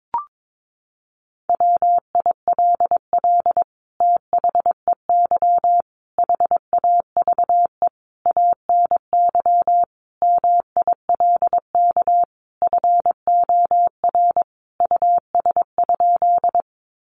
Morse code, heard as a keyed tone, about 22 wpm, 700 Hz. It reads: WILL THEY HAVE ANY MILK FOR US?
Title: Sentences from Top 1500 Words 22wpm